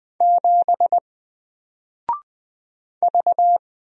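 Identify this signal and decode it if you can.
Morse code, heard as a keyed tone, about 20 wpm, 700 Hz. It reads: 7 V
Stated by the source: Single letter and number Morse drill at 20 wpm, from N4LCD.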